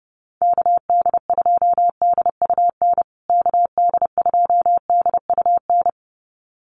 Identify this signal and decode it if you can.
Morse code, heard as a keyed tone, about 30 wpm, 700 Hz. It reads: KD2DUN KD2DUN
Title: USA Call Signs Repeated 2x 30wpm